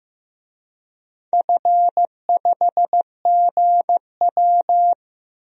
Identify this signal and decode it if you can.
Morse code, heard as a keyed tone, about 15 wpm, 700 Hz. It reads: F5GW
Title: DXCC-100 Call Signs - 4 Characters - Repeated 2x - 15wpm